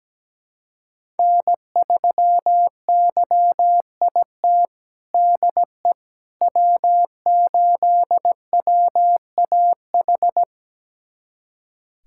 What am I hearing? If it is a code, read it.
Morse code, heard as a keyed tone, about 17 wpm, 700 Hz. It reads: N3YIT DE W8WAH
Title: USA Callsign DE Callsign - 5 Characters - 17wpm